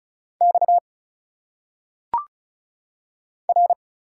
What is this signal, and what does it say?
Morse code, heard as a keyed tone, about 33 wpm, 700 Hz. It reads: X R
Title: Single Letters - Phonetic Pronunciation 35wpm